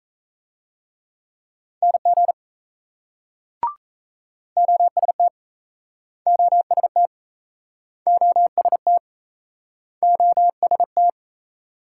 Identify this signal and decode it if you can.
Morse code, heard as a keyed tone, about 38 wpm, 700 Hz. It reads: NG OST OST OST OST